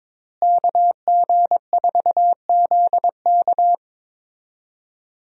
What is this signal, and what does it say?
Morse code, heard as a keyed tone, about 22 wpm, 700 Hz. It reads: KG4ZK
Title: USA Call Signs - 5 Characters - Repeated 3x - 22wpm